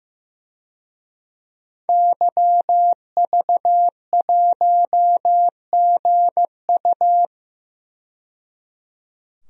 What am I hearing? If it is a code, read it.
Morse code, heard as a keyed tone, about 15 wpm, 700 Hz. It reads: YV1GU
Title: DXCC-100 Call Signs - 5 Characters - 15wpm